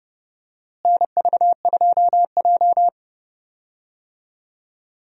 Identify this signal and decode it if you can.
Morse code, heard as a keyed tone, about 30 wpm, 700 Hz. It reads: NV2J
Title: USA Call Signs - 4 Characters - Repeated 3x - 30wpm